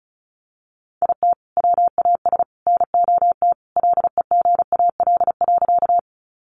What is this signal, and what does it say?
Morse code, heard as a keyed tone, about 34 wpm, 700 Hz. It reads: IT WAS NOT LEGAL.
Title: Sentences from Top 1200 Words 35wpm